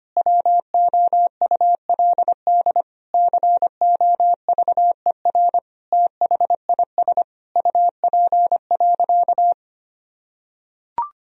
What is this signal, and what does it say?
Morse code, heard as a keyed tone, about 25 wpm, 700 Hz. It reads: WOULD COVER THIS UP.
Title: Sentences from Top 1000 Words - Review Part 1 - 25wpm